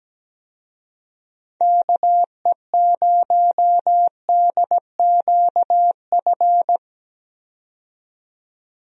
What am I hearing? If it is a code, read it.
Morse code, heard as a keyed tone, about 17 wpm, 700 Hz. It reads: KE0DQF
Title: USA Call Signs - 6 Characters - 17wpm